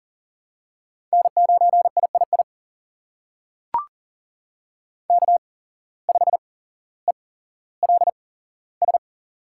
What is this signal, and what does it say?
Morse code, heard as a keyed tone, about 40 wpm, spaced as for 17 wpm, 700 Hz. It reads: N9III K5ELS
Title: USA Call Signs - 5 Characters - 8x Character Spacing - 40wpm